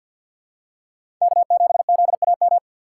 Morse code, heard as a keyed tone, about 53 wpm, 700 Hz. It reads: K7ZAM
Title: USA Call Signs - 5 Characters - 2x Character Spacing - 50wpm